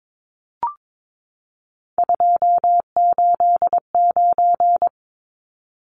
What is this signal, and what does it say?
Morse code, heard as a keyed tone, about 22 wpm, 700 Hz. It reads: 289
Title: Numbers - 3-Digit 22wpm